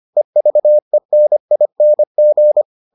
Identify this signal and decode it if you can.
Morse code, heard as a keyed tone, about 25 wpm, 600 Hz. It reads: EVENING